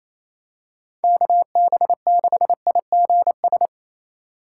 Morse code, heard as a keyed tone, about 28 wpm, 700 Hz. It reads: KB6IGS
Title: USA Call Signs - 6 Characters - 3x Character Spacing - 28wpm